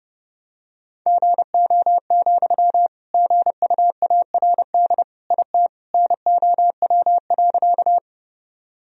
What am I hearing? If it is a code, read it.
Morse code, heard as a keyed tone, about 32 wpm, 700 Hz. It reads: GO, GUARD IT NOW.
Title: Sentences from Top 1000 Words - Review Part 4 - 30wpm